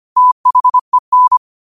Morse code, heard as a keyed tone, about 25 wpm, 1000 Hz. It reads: THEN